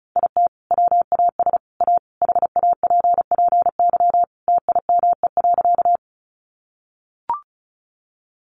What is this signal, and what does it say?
Morse code, heard as a keyed tone, about 34 wpm, 700 Hz. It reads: IT WAS A HAPPY TIME.
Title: Sentences from Top 1000 Words 35wpm